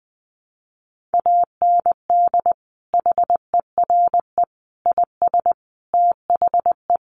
Morse code, heard as a keyed tone, about 20 wpm, 700 Hz. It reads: AND HERE IS THE
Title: Sentences from Top 500 Words 20wpm